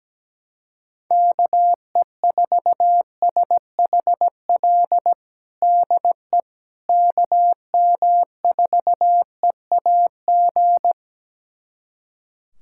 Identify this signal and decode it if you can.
Morse code, heard as a keyed tone, about 17 wpm, 700 Hz. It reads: KE4SHL DE KM4EAG
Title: USA Callsign DE Callsign - 6 Characters - 17wpm